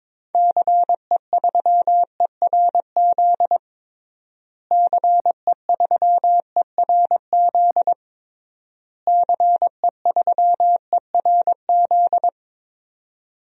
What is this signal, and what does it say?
Morse code, heard as a keyed tone, about 22 wpm, 700 Hz. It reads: CE3ERZ CE3ERZ CE3ERZ
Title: DXCC-100 Call Signs - 6 Characters - Repeated 3x - 22wpm